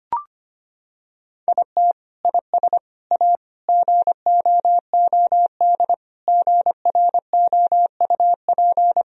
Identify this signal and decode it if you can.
Morse code, heard as a keyed tone, about 25 wpm, 700 Hz. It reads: IT IS A GOOD GROUP
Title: Sentences from Top 1000 Words - Review Part 1 - 25wpm